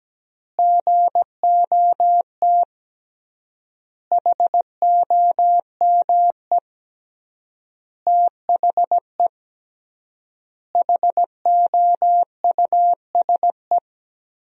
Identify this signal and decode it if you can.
Morse code, heard as a keyed tone, about 17 wpm, 700 Hz. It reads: GOT HOME THE HOUSE
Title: 3x Word-Spacing - Sentences from Top 500 Words - 17wpm